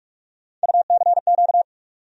Morse code, heard as a keyed tone, about 42 wpm, 700 Hz. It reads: UCQ
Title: USA Call Sign Postfixes - 3 Letters 45wpm